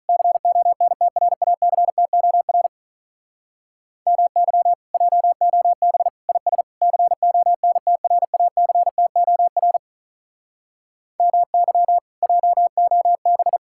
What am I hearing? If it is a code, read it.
Morse code, heard as a keyed tone, about 46 wpm, 700 Hz. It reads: CONTRACTOR MY JOB IS CONTRACTOR MY JOB